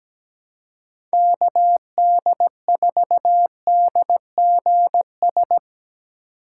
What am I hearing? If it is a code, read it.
Morse code, heard as a keyed tone, about 17 wpm, 700 Hz. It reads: KD4DGS
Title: USA Call Signs - 6 Characters - 2x Character Spacing - 17wpm